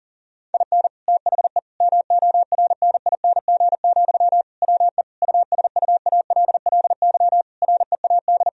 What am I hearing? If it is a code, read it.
Morse code, heard as a keyed tone, about 40 wpm, 700 Hz. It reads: IN THE MORNING, WE USUALLY READ
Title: Sentences from Top 1000 Words - Review Part 3 - 40wpm